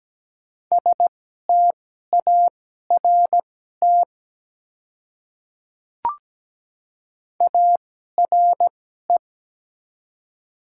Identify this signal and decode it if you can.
Morse code, heard as a keyed tone, about 17 wpm, spaced as for 12 wpm, 700 Hz. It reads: START ARE
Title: 2x Character Spacing - Top 300 Words - 17wpm